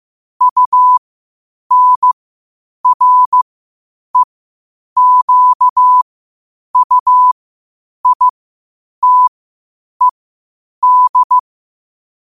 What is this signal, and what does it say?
Morse code, heard as a keyed tone, about 15 wpm, spaced as for 8 wpm, 1000 Hz. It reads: UNREQUITED
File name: qotdfiles / 8wpm